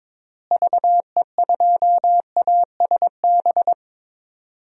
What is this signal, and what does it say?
Morse code, heard as a keyed tone, about 22 wpm, 700 Hz. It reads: VE2ASB
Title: DXCC-100 Call Signs - 6 Characters - Repeated 3x - 22wpm